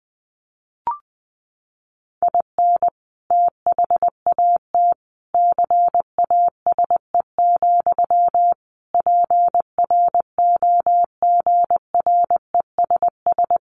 Morse code, heard as a keyed tone, about 19 wpm, 700 Hz. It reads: IN THAT CASE, PROGRESS
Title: Sentences from Top 1400 Words 20wpm